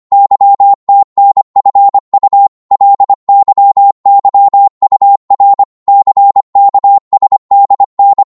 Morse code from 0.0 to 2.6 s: YTNFU